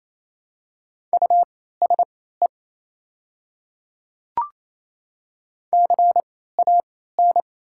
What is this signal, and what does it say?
Morse code, heard as a keyed tone, about 28 wpm, spaced as for 16 wpm, 700 Hz. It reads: USE CAN
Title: 3x Character Spacing - Top 100 Words - 28wpm